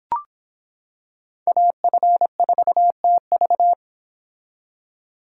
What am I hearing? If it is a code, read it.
Morse code, heard as a keyed tone, about 26 wpm, 700 Hz. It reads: AF4TV